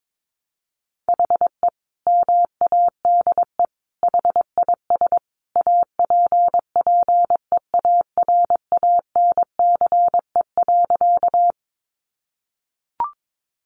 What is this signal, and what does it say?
Morse code, heard as a keyed tone, about 22 wpm, 700 Hz. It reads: HE MADE HIS APPEARANCE.
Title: Sentences from Top 1100 Words 22wpm